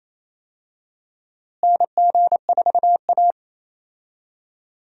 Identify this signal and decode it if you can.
Morse code, heard as a keyed tone, about 28 wpm, 700 Hz. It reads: NG4A
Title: USA Call Signs - 4 Characters - Repeated 3x - 28wpm